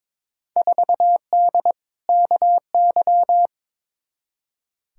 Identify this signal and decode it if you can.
Morse code, heard as a keyed tone, about 22 wpm, 700 Hz. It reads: 4D KY
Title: Field Day Exchanges Encore - 22wpm